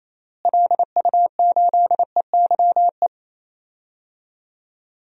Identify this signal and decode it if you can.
Morse code, heard as a keyed tone, about 28 wpm, 700 Hz. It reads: LU8EYE
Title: DXCC-100 Call Signs - 6 Characters - Repeated 2x - 28wpm